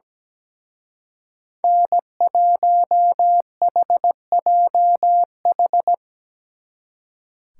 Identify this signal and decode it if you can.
Morse code, heard as a keyed tone, about 19 wpm, 700 Hz. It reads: N1HJH
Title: Speed Racing - USA 1x3 Call Signs 17wpm